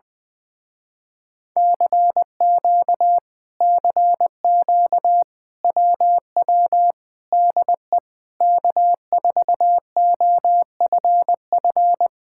Morse code, heard as a keyed tone, about 20 wpm, 700 Hz. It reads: CQ CQ WW DE K4OFF